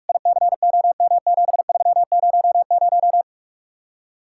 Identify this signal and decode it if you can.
Morse code, heard as a keyed tone, about 45 wpm, 700 Hz. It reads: ICOM7300